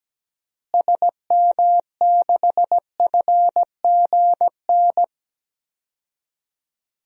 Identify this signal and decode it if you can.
Morse code, heard as a keyed tone, about 17 wpm, 700 Hz. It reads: SM6FGN
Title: DXCC-100 Call Signs - 6 Characters - Repeated 3x - 17wpm